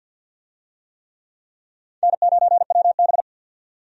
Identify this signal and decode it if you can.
Morse code, heard as a keyed tone, about 50 wpm, 700 Hz. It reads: N9WB
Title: USA Call Signs - 4 Characters - 50wpm